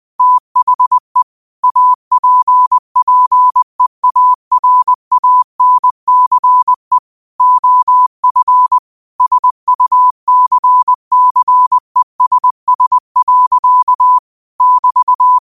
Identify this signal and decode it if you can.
Morse code, heard as a keyed tone, about 20 wpm, 1000 Hz. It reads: THE APPEARANCE OF SUCCESS. <BT>
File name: qotdfiles / 20wpm